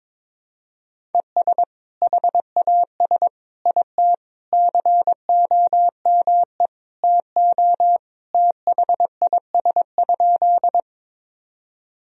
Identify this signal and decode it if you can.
Morse code, heard as a keyed tone, about 22 wpm, 700 Hz. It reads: ES HAS IT COME TO THIS?
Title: Sentences from Top 100 Words and Abbreviations 22wpm